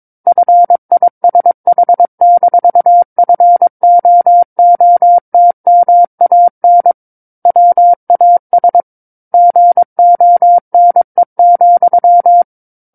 Morse code, heard as a keyed tone, about 22 wpm, 700 Hz. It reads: FISH-FOOTMAN WAS GONE,
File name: 06-Chapter-06 22 WPM